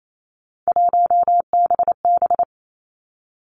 Morse code, heard as a keyed tone, about 28 wpm, 700 Hz. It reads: 1BB
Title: Three Challenging Letter-Number Combinations 28wpm